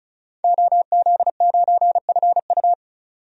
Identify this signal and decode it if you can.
Morse code, heard as a keyed tone, about 34 wpm, 700 Hz. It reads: OZ9FU
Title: DXCC-100 Call Signs - 5 Characters - 35wpm